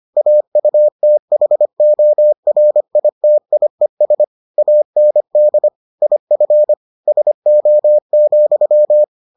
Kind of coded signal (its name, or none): Morse code